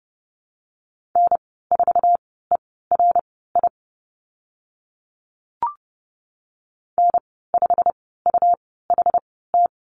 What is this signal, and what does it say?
Morse code, heard as a keyed tone, about 30 wpm, spaced as for 17 wpm, 700 Hz. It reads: N4ERI N5UHT
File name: Lesson 19g - Letter w - Call Signs 30wpm